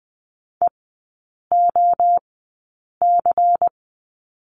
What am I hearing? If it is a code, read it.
Morse code, heard as a keyed tone, about 20 wpm, spaced as for 8 wpm, 700 Hz. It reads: EOC
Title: 107 Lesson 18e - Letter m - Three Characters 20wpm